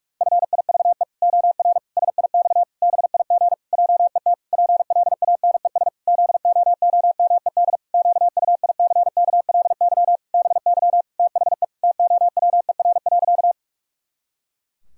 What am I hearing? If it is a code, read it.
Morse code, heard as a keyed tone, about 44 wpm, 700 Hz. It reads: FIVE OR SIX BIG JET PLANES ZOOMED QUICKLY BY THE TOWER.
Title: Pangram Challenge v2 - 45wpm